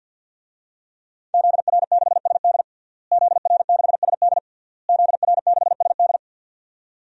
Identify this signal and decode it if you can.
Morse code, heard as a keyed tone, about 52 wpm, 700 Hz. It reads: ZR6SD ZR6SD ZR6SD
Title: DXCC-100 Call Signs - 5 Characters - Repeated 3x - 50wpm